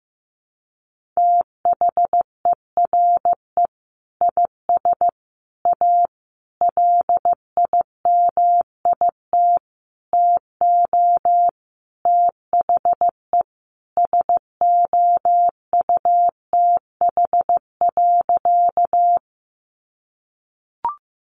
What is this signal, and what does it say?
Morse code, heard as a keyed tone, about 15 wpm, 700 Hz. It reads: THERE IS A LIMIT TO THE SOUTH.
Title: Sentences from Top 1500 Words 15wpm